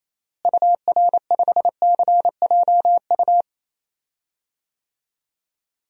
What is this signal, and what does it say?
Morse code, heard as a keyed tone, about 27 wpm, 700 Hz. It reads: UR5CJU
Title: DXCC-100 Call Signs - 6 Characters - Repeated 2x - 28wpm